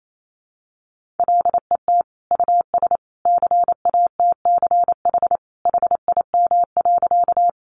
Morse code, heard as a keyed tone, about 28 wpm, 700 Hz. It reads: LET US CATCH HIM.
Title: Sentences from Top 1500 Words 28wpm